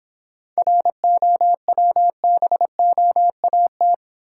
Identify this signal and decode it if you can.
Morse code, heard as a keyed tone, about 26 wpm, 700 Hz. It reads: ROWBOAT